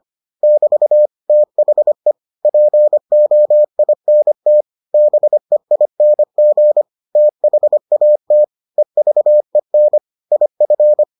Morse code, heard as a keyed tone, about 25 wpm, 600 Hz. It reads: <BT> THE POINT BEING THAT EVEN IF